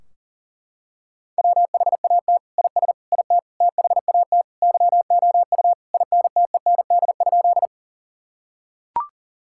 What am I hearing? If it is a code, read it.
Morse code, heard as a keyed tone, about 40 wpm, 700 Hz. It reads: WHAT IS IT THAT YOU INTEND?
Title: Sentences from Top 1600 Words 40wpm